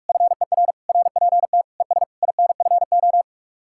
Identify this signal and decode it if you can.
Morse code, heard as a keyed tone, about 58 wpm, 700 Hz. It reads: FER RPT ES INFO